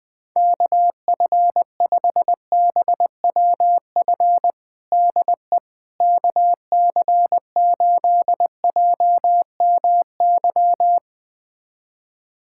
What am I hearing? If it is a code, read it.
Morse code, heard as a keyed tone, about 20 wpm, 700 Hz. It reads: KF5BWF DE KC8JMY
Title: USA Callsign DE Callsign - 6 Characters - 20wpm